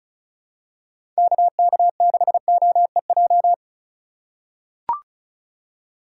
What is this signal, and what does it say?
Morse code, heard as a keyed tone, about 34 wpm, 700 Hz. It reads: KK6OEJ
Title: USA Call Signs - 6 Characters - 2x Character Spacing - 35wpm